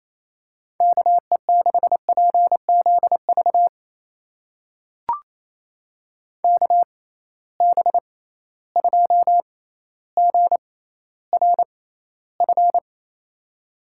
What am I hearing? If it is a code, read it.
Morse code, heard as a keyed tone, about 28 wpm, spaced as for 14 wpm, 700 Hz. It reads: KE6PZV KB2GRF
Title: USA Call Signs - 6 Characters - 6x Character Spacing - 28wpm